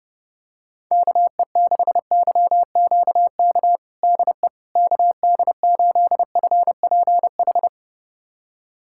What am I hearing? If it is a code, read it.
Morse code, heard as a keyed tone, about 32 wpm, 700 Hz. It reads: KE6YQK DE KD8FPH
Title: USA Callsign DE Callsign 30wpm